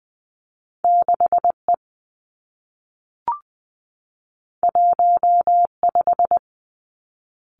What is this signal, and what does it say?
Morse code, heard as a keyed tone, about 20 wpm, 700 Hz. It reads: 6 E 1 5
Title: Two Challenging Letter-Number Combinations 20wpm